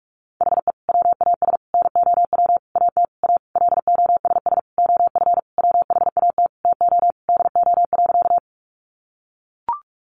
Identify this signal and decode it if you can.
Morse code, heard as a keyed tone, about 44 wpm, 700 Hz. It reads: HE WAS NOW AT A LOSS OF WHAT TO DO.
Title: Sentences from Top 1400 Words 45wpm